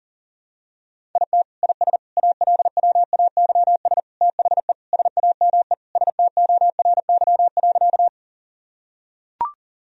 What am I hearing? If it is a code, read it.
Morse code, heard as a keyed tone, about 40 wpm, 700 Hz. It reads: IT IS ALWAYS THE SAME STORY.
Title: Sentences from Top 1000 Words - Review Part 1 - 40wpm